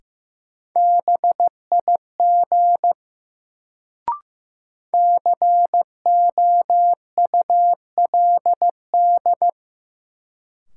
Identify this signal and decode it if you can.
Morse code, heard as a keyed tone, about 15 wpm, 700 Hz. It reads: BIG COULD
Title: Sets of 2 Words - New Word Drill - 15wpm